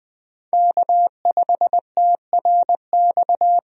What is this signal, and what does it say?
Morse code, heard as a keyed tone, about 21 wpm, 700 Hz. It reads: K5TRX
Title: USA Call Signs - 5 Characters - 20wpm